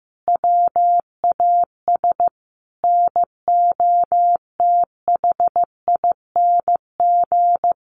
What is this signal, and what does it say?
Morse code, heard as a keyed tone, about 15 wpm, 700 Hz. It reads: WAS NOTHING
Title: Sentences from Top 1000 Words 15wpm